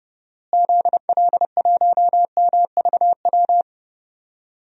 Morse code, heard as a keyed tone, about 30 wpm, 700 Hz. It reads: ZL1MVW
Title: DXCC-100 Call Signs - 6 Characters - Repeated 3x - 30wpm